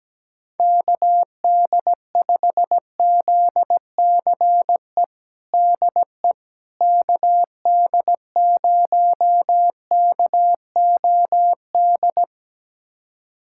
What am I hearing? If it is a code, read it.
Morse code, heard as a keyed tone, about 17 wpm, 700 Hz. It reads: KD5ZCE DE KD0KOD
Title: USA Callsign DE Callsign - 6 Characters - 17wpm